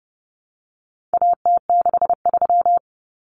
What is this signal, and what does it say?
Morse code, heard as a keyed tone, about 30 wpm, 700 Hz. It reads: AT63
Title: Common Cut Numbers - 4-Digit 30wpm